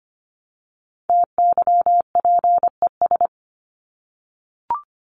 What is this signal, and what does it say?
Morse code, heard as a keyed tone, about 25 wpm, 700 Hz. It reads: TYPES
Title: ES:S Suffix - Top 100 Words with Base Words 25wpm